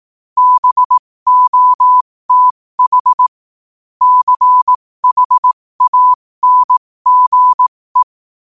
Morse code, heard as a keyed tone, about 18 wpm, spaced as for 15 wpm, 1000 Hz. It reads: BOTH CHANGE
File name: qotdfiles / 15wpm